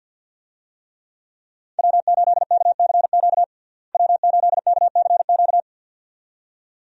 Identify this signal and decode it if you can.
Morse code, heard as a keyed tone, about 48 wpm, 700 Hz. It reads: W8KCQ W8KCQ
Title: USA Call Signs - 5 Characters - Repeated 2x - 50wpm